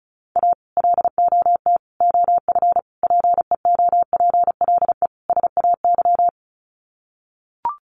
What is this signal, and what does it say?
Morse code, heard as a keyed tone, about 34 wpm, 700 Hz. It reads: A LOT OF PEOPLE SAY
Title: Sets of 5 Words Encore2 35wpm